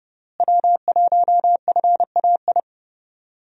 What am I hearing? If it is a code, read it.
Morse code, heard as a keyed tone, about 32 wpm, 700 Hz. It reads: W1FAI
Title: USA Call Signs - 5 Characters - Repeated 2x - 30wpm